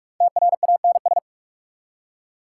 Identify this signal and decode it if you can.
Morse code, heard as a keyed tone, about 46 wpm, 690 Hz. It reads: TRANS
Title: Prefixes and Suffixes 45wpm